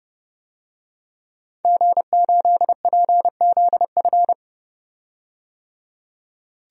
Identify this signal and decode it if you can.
Morse code, heard as a keyed tone, about 31 wpm, 700 Hz. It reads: G8PZF
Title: DXCC-100 Call Signs - 5 Characters - 30wpm